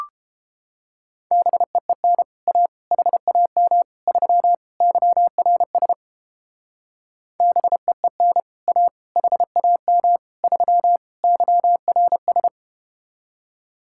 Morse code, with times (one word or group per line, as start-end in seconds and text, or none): 1.3-2.2 BEEN
2.5-2.7 A
2.9-3.8 HAM
4.1-4.5 3
4.8-5.9 YRS
7.4-8.4 BEEN
8.7-8.9 A
9.2-10.2 HAM
10.4-11.0 3
11.2-12.5 YRS